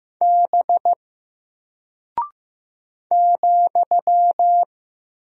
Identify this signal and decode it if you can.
Morse code, heard as a keyed tone, about 15 wpm, 700 Hz. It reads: B ,
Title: Single Character (Letter:Number:Punctuation:Prosign) Alt 15wpm